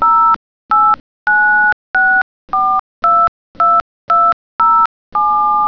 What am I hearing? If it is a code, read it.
DTMF, heard as telephone touch tones, tones of about 290 ms. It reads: 0896422207